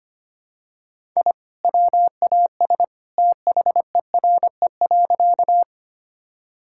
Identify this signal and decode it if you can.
Morse code, heard as a keyed tone, about 25 wpm, 700 Hz. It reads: I WAS THERE.
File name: practice / 25wpm / Sentences with 2-Word Phrases - 25wpm